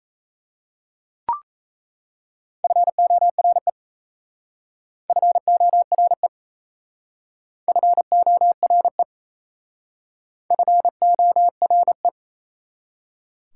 Morse code, heard as a keyed tone, about 34 wpm, 700 Hz. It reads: FORE FORE FORE FORE
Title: Speed-Racing Prefixes and Suffixes 28wpm